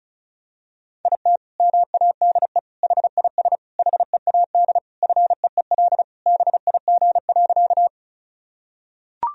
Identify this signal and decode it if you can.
Morse code, heard as a keyed tone, about 34 wpm, 700 Hz. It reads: IT MADE HIS HEAD FEEL BIG.
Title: Sentences from Top 1000 Words - Review Part 1 - 35wpm